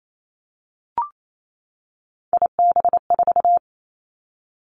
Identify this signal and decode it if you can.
Morse code, heard as a keyed tone, about 28 wpm, 700 Hz. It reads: IB4
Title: Three Challenging Letter-Number Combinations 28wpm